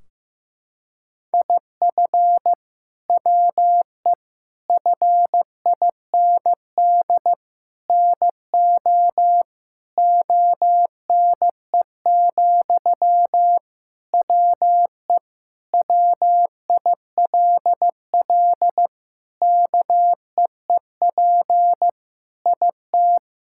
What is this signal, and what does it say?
Morse code, heard as a keyed tone, about 15 wpm, 700 Hz. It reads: IF WE FIND NO ONE, WE WILL KEEP IT
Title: Sentences from Top 1600 Words 15wpm